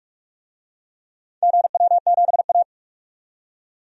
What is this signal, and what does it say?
Morse code, heard as a keyed tone, about 47 wpm, 700 Hz. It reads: GW7A
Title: DXCC-100 Call Signs - 4 Characters - 45wpm